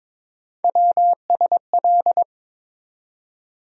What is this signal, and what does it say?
Morse code, heard as a keyed tone, about 22 wpm, 700 Hz. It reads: WSL
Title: USA Call Sign Postfixes - 3 Letters 22wpm